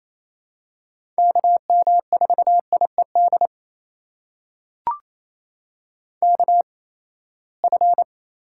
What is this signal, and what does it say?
Morse code, heard as a keyed tone, about 28 wpm, spaced as for 19 wpm, 700 Hz. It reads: KM4IED KF
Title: USA Call Signs - 6 Characters - 8x Character Spacing - 28wpm